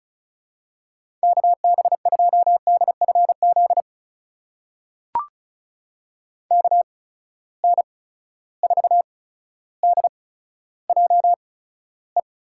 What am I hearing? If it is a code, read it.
Morse code, heard as a keyed tone, about 36 wpm, spaced as for 15 wpm, 700 Hz. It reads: KB2DFZ KN4DJE
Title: USA Call Signs - 6 Characters - 8x Character Spacing - 35wpm